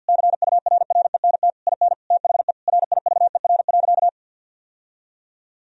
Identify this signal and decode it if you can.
Morse code, heard as a keyed tone, about 50 wpm, 700 Hz. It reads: CURRENT IN THE RIVER.
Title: Sentences from Top 1000 Words - Review Part 4 - 50wpm